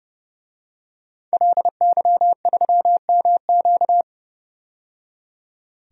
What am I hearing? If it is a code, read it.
Morse code, heard as a keyed tone, about 31 wpm, 700 Hz. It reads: LY3MQ
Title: DXCC-100 Call Signs - 5 Characters - 30wpm